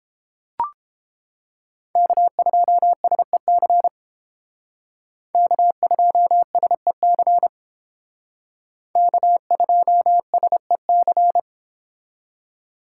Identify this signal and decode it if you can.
Morse code, heard as a keyed tone, about 31 wpm, 700 Hz. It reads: K2SEC K2SEC K2SEC